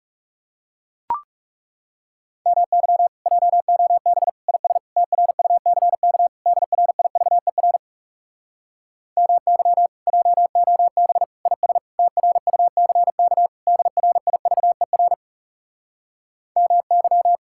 Morse code, heard as a keyed tone, about 37 wpm, 700 Hz. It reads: MY JOB IS TRUCK DRIVER MY JOB IS TRUCK DRIVER MY
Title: Speed-Racing Common QSO Phrases Encore - 30wpm